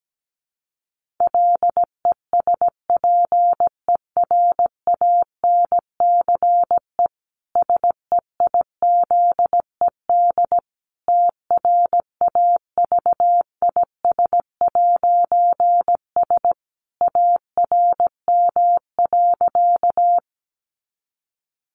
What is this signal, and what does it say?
Morse code, heard as a keyed tone, about 17 wpm, 700 Hz. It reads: LESPERANCE SEIZED TRAVIS'S ARM.
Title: A Sound Of Thunder 17wpm